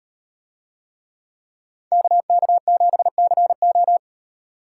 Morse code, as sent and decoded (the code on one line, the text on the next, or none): -.- -.- --... -.-. ---
KK7CO